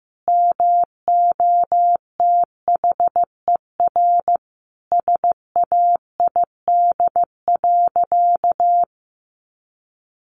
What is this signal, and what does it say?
Morse code, heard as a keyed tone, about 15 wpm, 700 Hz. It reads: MOTHER SAID.
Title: Sentences from Top 700 Words 15wpm